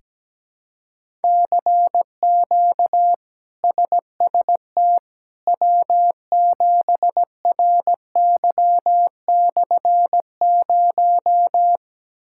Morse code, heard as a keyed tone, about 17 wpm, 700 Hz. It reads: CQ SST W7RY/0
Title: K1USN SST QSO Practice - 17wpm